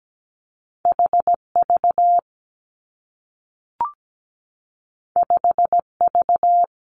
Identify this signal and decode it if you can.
Morse code, heard as a keyed tone, about 17 wpm, 700 Hz. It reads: H V 5 V
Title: Two Challenging Letter-Number Combinations 17wpm